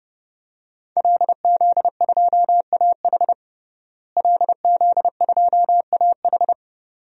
Morse code, heard as a keyed tone, about 30 wpm, 700 Hz. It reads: LZ2AH LZ2AH
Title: DXCC-100 Call Signs - 5 Characters - Repeated 2x - 30wpm